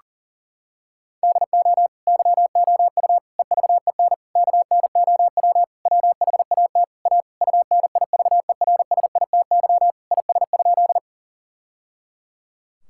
Morse code, heard as a keyed tone, about 40 wpm, 700 Hz. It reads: DO YOU EVEN KNOW WHAT A UNIVERSITY IS?
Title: Sentences from Top 1000 Words - Review Part 5 - 40wpm